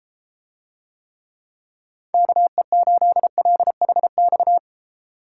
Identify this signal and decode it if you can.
Morse code, heard as a keyed tone, about 33 wpm, 700 Hz. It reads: KE8LHX